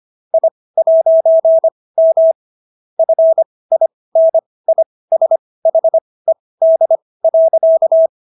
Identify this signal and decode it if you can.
Morse code, heard as a keyed tone, about 25 wpm, spaced as for 18 wpm, 650 Hz. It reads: I'M FINISHED.